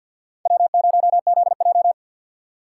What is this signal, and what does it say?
Morse code, heard as a keyed tone, about 52 wpm, 700 Hz. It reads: W0ZJ
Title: USA Call Signs - 4 Characters - 2x Character Spacing - 50wpm